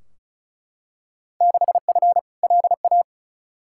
Morse code, heard as a keyed tone, about 36 wpm, 700 Hz. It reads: 6F LA
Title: Field Day Exchanges Encore - 35wpm